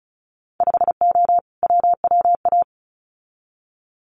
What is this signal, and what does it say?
Morse code, heard as a keyed tone, about 36 wpm, 700 Hz. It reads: 5O WWA
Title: Winter Field Day Exchanges Encore 35wpm